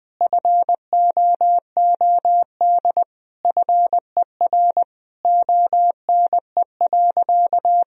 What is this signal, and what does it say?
Morse code, heard as a keyed tone, about 20 wpm, 700 Hz. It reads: FOOD FER ONE.
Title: Sentences from Top 300 Words and Abbreviations 20wpm